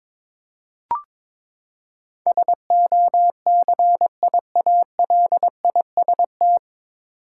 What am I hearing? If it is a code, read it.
Morse code, heard as a keyed tone, about 22 wpm, 700 Hz. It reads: SOCIALIST
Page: Morse Code Ninja practice 22wpm